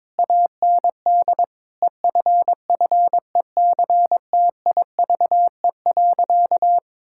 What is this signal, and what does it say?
Morse code, heard as a keyed tone, about 21 wpm, 700 Hz. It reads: AND EFFECTIVE.
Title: Sentences from Top 1000 Words - Review Part 6 - 22wpm